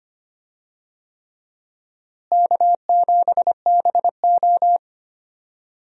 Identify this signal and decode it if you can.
Morse code, heard as a keyed tone, about 25 wpm, 700 Hz. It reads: K7BO
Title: USA Call Signs - 4 Characters - Repeated 3x - 25wpm